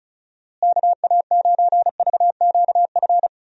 Morse code, heard as a keyed tone, about 34 wpm, 700 Hz. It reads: KA9VQF
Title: USA Call Signs - 6 Characters - Repeated 2x - 35wpm